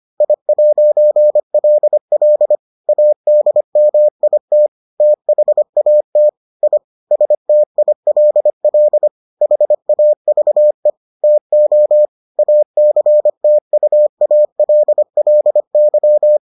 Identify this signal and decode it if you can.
Morse code, heard as a keyed tone, about 25 wpm, 600 Hz. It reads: I'LL ADMIT THAT I STILL HAVE TO ACTUALLY